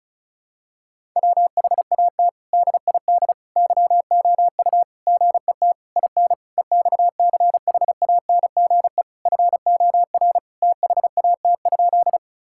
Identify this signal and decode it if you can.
Morse code, heard as a keyed tone, about 36 wpm, 700 Hz. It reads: WHAT DID YOU GET IN EXCHANGE FOR THAT?
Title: Sentences from Top 1000 Words - Review Part 6 - 35wpm